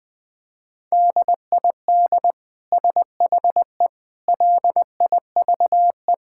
Morse code, heard as a keyed tone, about 20 wpm, 700 Hz. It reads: DID SHE LIVE
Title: Sentences from Top 1000 Words - Review Part 5 - 20wpm